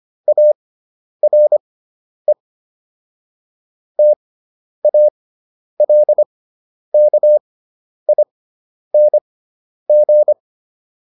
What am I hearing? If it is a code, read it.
Morse code, heard as a keyed tone, about 25 wpm, spaced as for 10 wpm, 600 Hz. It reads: ARE TALKING